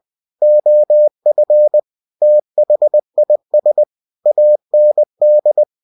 Morse code, heard as a keyed tone, about 20 wpm, 600 Hz. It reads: OF THIS AND
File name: winnie-20wpm